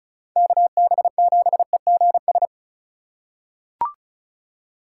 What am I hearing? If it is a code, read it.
Morse code, heard as a keyed tone, about 34 wpm, 700 Hz. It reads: KB7EGS
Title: USA Call Signs - 6 Characters - 3x Character Spacing - 35wpm